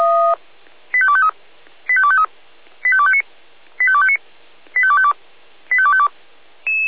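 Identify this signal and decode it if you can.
Five-tone selective call (ZVEI), heard as quick runs of five tones, tones of about 70 ms. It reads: 85253 85253 85259 85259 85252 85252
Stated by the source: Five-tone ZVEI selective call 85218, received on SDR.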